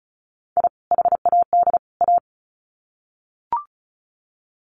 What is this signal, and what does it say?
Morse code, heard as a keyed tone, about 34 wpm, 700 Hz. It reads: I HAD A
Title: Sets of 3 Words 35wpm